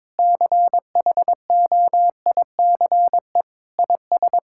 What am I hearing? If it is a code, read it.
Morse code, heard as a keyed tone, about 22 wpm, 700 Hz. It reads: CHOICE IS
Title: Sentences from Top 1000 Words - Review Part 6 - 22wpm